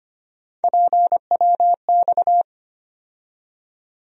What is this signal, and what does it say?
Morse code, heard as a keyed tone, about 25 wpm, 700 Hz. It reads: PWX
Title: USA Call Sign Postfixes - 3 Letters 25wpm